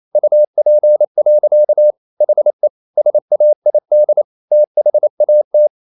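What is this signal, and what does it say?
Morse code, heard as a keyed tone, about 28 wpm, 600 Hz. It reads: UP. HE SAID THAT
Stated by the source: Morse code practice at 28 wpm, text of Ken Follett's novel Never in English, from CW med Gnister.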